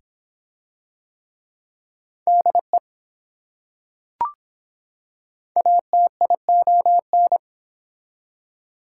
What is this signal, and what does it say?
Morse code, heard as a keyed tone, about 23 wpm, 700 Hz. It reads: DE ATION